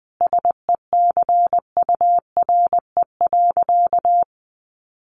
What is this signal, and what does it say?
Morse code, heard as a keyed tone, about 20 wpm, 700 Hz. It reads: SECURE.
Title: Sentences from Top 1500 Words 20wpm